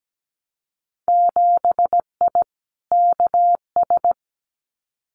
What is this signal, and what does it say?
Morse code, heard as a keyed tone, about 17 wpm, 700 Hz. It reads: 7I KS
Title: Winter Field Day Exchanges Encore 17wpm